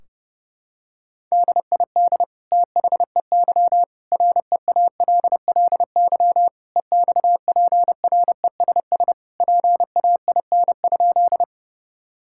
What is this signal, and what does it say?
Morse code, heard as a keyed tone, about 32 wpm, 700 Hz. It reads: DID THEY REALLY EXPRESS PAIN?
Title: Sentences from Top 1600 Words 30wpm